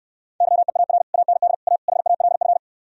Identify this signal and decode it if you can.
Morse code, heard as a keyed tone, about 67 wpm, 700 Hz. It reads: YEAR AND A HALF